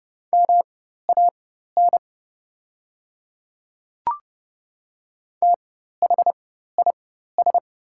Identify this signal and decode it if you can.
Morse code, heard as a keyed tone, about 30 wpm, spaced as for 14 wpm, 700 Hz. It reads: MAN THIS
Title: 4x Character Spacing - Top 300 Words - 30wpm